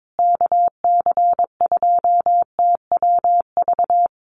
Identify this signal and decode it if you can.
Morse code, heard as a keyed tone, about 22 wpm, 700 Hz. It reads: KC2TWV
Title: USA Call Signs Repeated 3x 22wpm